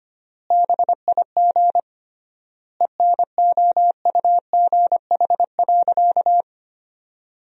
Morse code, heard as a keyed tone, about 25 wpm, 700 Hz. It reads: BIG ENOUGH.
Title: 3x Word-Spacing - Sentences from Top 400 Words - 25wpm